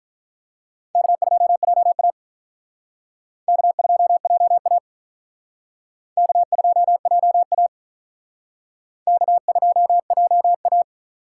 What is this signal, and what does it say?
Morse code, heard as a keyed tone, about 47 wpm, 700 Hz. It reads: K2JA K2JA K2JA K2JA